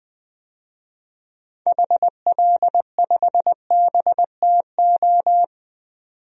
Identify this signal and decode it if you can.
Morse code, heard as a keyed tone, about 20 wpm, 700 Hz. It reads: HL5BTO
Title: DXCC-100 Call Signs - 6 Characters - 20wpm